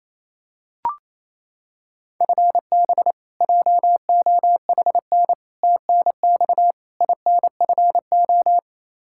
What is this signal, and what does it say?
Morse code, heard as a keyed tone, about 28 wpm, 700 Hz. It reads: FB JOHN TNX INFO
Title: Practice QSO - Exchange 3 - More Info - 28wpm